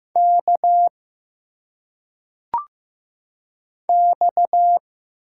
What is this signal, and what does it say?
Morse code, heard as a keyed tone, about 15 wpm, 700 Hz. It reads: K X
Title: Single Letters - Phonetic Pronunciation 15wpm